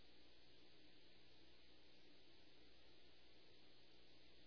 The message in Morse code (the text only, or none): none